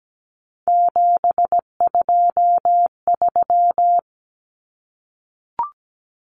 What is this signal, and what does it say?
Morse code, heard as a keyed tone, about 17 wpm, 700 Hz. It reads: 723
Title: Common Cut Numbers - 3-Digit 17wpm